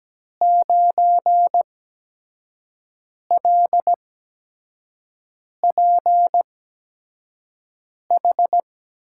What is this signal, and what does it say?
Morse code, heard as a keyed tone, about 17 wpm, spaced as for 5 wpm, 700 Hz. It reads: 9LPH
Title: USA Call Signs - 6 Characters - 8x Character Spacing - 17wpm